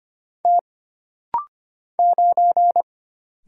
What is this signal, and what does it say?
Morse code, heard as a keyed tone, about 25 wpm, 700 Hz. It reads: T 9
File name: Single Character (Letter:Number:Punctuation:Prosign) - Rapid-Fire - 25wpm